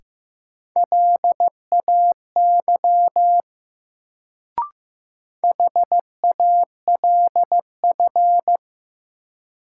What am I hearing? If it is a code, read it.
Morse code, heard as a keyed tone, about 15 wpm, 700 Hz. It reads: LAY HALF
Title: Sentences from Top 400 Words - New Word Drill - 15wpm